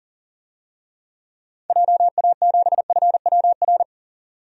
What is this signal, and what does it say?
Morse code, heard as a keyed tone, about 40 wpm, 700 Hz. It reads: JA7FWR
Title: DXCC-100 Call Signs - 6 Characters - 40wpm